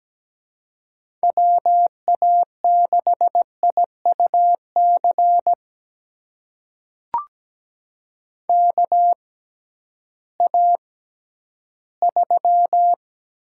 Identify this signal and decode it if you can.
Morse code, heard as a keyed tone, about 17 wpm, spaced as for 11 wpm, 700 Hz. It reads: WA6IUC KA3
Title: USA Call Signs - 6 Characters - 6x Character Spacing - 17wpm